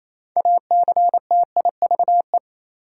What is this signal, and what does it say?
Morse code, heard as a keyed tone, about 28 wpm, 700 Hz. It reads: ACTIVE
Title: Sentences from Top 900 Words - New Word Drill - 28wpm